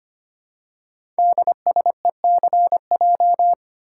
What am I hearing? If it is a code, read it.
Morse code, heard as a keyed tone, about 25 wpm, 700 Hz. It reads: DSECJ
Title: Code Groups - 5 Letters - 25wpm